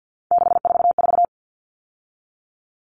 Morse code, heard as a keyed tone, about 50 wpm, 700 Hz. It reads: 644
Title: Numbers - 3-Digit 50wpm